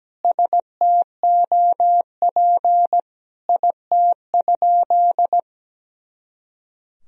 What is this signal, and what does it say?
Morse code, heard as a keyed tone, about 17 wpm, 700 Hz. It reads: STOP IT?
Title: Sentences from Top 300 Words and Abbreviations 17wpm